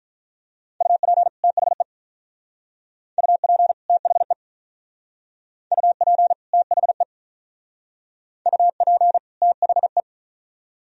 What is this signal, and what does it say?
Morse code, heard as a keyed tone, about 45 wpm, 700 Hz. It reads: UP THE UP THE UP THE UP THE